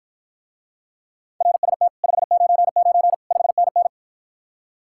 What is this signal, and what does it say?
Morse code, heard as a keyed tone, about 54 wpm, 700 Hz. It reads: RST 599 5NN